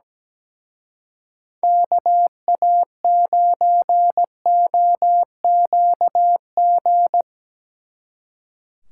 Morse code, heard as a keyed tone, about 18 wpm, 700 Hz. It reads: KA9OQG
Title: Speed Racing - USA 2x3 Call Signs 17wpm